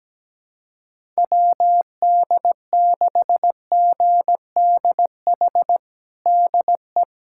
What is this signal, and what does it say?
Morse code, heard as a keyed tone, about 17 wpm, 700 Hz. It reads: WD6GDH DE
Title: USA Callsign DE Callsign - 6 Characters - 17wpm